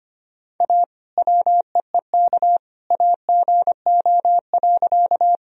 Morse code, heard as a keyed tone, about 25 wpm, 700 Hz. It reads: A WEEK AGO.
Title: Sentences from Top 1000 Words - Review Part 3 - 25wpm